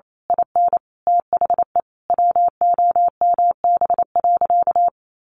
Morse code, heard as a keyed tone, about 28 wpm, 700 Hz. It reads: IN THE WOMB.
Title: A Sound Of Thunder 28wpm